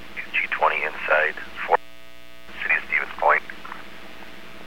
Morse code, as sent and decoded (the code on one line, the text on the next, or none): none